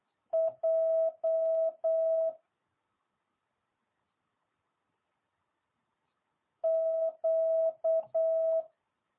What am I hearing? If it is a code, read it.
Morse code, heard as a keyed tone, about 8 wpm, 650 Hz. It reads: J Q